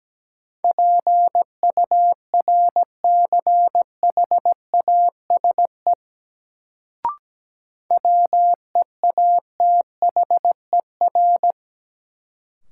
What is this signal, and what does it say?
Morse code, heard as a keyed tone, about 17 wpm, 700 Hz. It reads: PURCHASE WEATHER
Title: Sentences from Top 800 Words - New Word Drill - 17wpm